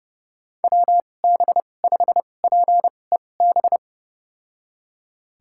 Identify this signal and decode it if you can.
Morse code, heard as a keyed tone, about 31 wpm, spaced as for 22 wpm, 700 Hz. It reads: WB5PEB
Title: USA Call Signs - 6 Characters - 2x Character Spacing - 30wpm